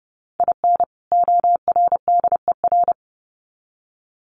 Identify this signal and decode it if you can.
Morse code, heard as a keyed tone, about 30 wpm, 700 Hz. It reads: IN ORDER
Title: Sets of 2 Words 30wpm